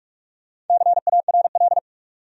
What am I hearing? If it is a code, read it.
Morse code, heard as a keyed tone, about 47 wpm, 700 Hz. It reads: CARL
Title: USA Common Names - Max 5 Letters - 45wpm